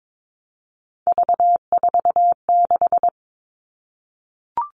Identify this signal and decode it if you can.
Morse code, heard as a keyed tone, about 22 wpm, 700 Hz. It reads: V46
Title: All Cut Numbers - 3-Digit 22wpm